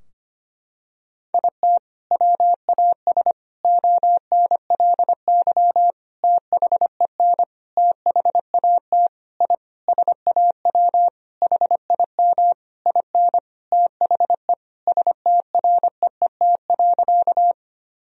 Morse code, heard as a keyed tone, about 25 wpm, 700 Hz. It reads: IT WAS ONLY THEN THAT I SAW HIM IN THE STREET.
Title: Sentences from Top 1000 Words - Review Part 3 - 25wpm